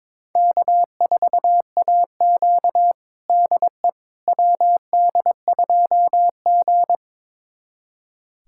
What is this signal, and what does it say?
Morse code, heard as a keyed tone, about 22 wpm, 700 Hz. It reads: K4AQ DE WD2G
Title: USA Callsign DE Callsign - 4 Characters - 22wpm